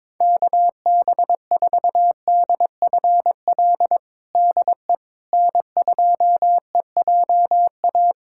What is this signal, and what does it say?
Morse code, heard as a keyed tone, about 22 wpm, 700 Hz. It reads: KB4DFL DE N2EJA
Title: USA Callsign DE Callsign 22wpm